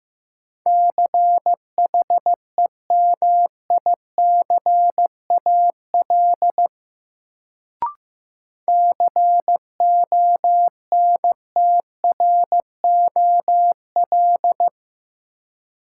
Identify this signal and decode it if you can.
Morse code, heard as a keyed tone, about 15 wpm, 700 Hz. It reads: CHEMICAL CONTROL
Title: Sentences from Top 1200 Words - New Word Drill Alt - 15wpm